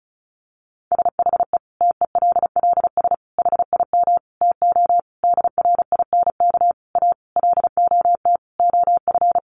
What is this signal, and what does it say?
Morse code, heard as a keyed tone, about 34 wpm, 700 Hz. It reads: SHE TELLS HIM TO DRINK A LOT OF
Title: Additional Easy Phrases 35wpm